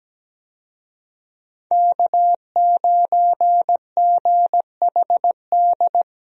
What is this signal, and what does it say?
Morse code, heard as a keyed tone, about 17 wpm, 700 Hz. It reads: K9GHD
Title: USA Call Signs - 5 Characters - 17wpm